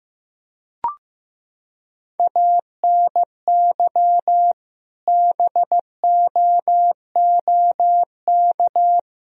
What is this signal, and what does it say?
Morse code, heard as a keyed tone, about 15 wpm, 700 Hz. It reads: ANY BOOK
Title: Sentences from Top 300 Words and Abbreviations 15wpm